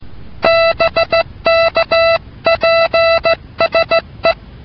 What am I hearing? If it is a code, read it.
Morse code, heard as a keyed tone, about 16 wpm, 660 Hz. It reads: BK PSE